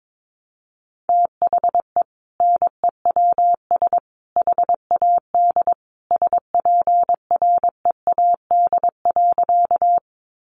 Morse code, heard as a keyed tone, about 22 wpm, 700 Hz. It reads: THE NEWS HAD SPREAD.
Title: Sentences from Top 1200 Words 22wpm